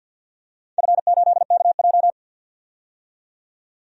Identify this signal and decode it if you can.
Morse code, heard as a keyed tone, about 51 wpm, 700 Hz. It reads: F8KJ
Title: DXCC-100 Call Signs - 4 Characters - 50wpm